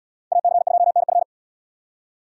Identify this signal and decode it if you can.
Morse code, heard as a keyed tone, about 77 wpm, 700 Hz. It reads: AC2NF